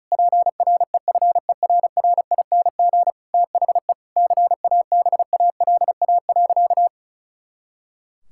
Morse code, heard as a keyed tone, about 36 wpm, 700 Hz. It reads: PREFERRING THE CABALA.
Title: Pangram Challenge v2 - 35wpm